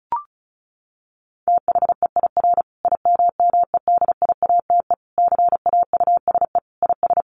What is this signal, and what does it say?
Morse code, heard as a keyed tone, about 34 wpm, 700 Hz. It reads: THEIR IMMEDIATE CAUSE IS
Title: Sentences from Top 1200 Words 35wpm